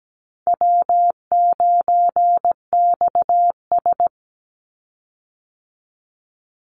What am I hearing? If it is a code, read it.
Morse code, heard as a keyed tone, about 17 wpm, 700 Hz. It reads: W9XS
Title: Easy Call Signs Repeated 2x - 17wpm